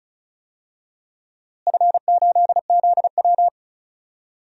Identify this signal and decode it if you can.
Morse code, heard as a keyed tone, about 36 wpm, 700 Hz. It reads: F8ZW
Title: DXCC-100 Call Signs - 4 Characters - Repeated 3x - 35wpm